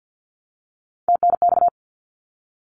Morse code, heard as a keyed tone, about 50 wpm, 700 Hz. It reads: TNX
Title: Abbreviations 50wpm